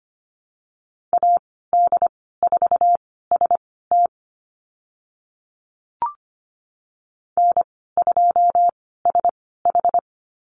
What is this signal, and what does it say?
Morse code, heard as a keyed tone, about 25 wpm, spaced as for 16 wpm, 690 Hz. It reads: AD4ST N2SH
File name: Lesson 15g - Number 5 - Call Signs 25wpm